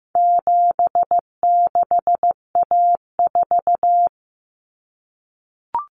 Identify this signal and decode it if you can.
Morse code, heard as a keyed tone, about 15 wpm, 700 Hz. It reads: 76A4
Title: Common Cut Numbers - 4-Digit 15wpm